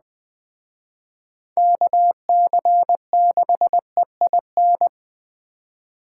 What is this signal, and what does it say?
Morse code, heard as a keyed tone, about 21 wpm, 700 Hz. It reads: KC6EIN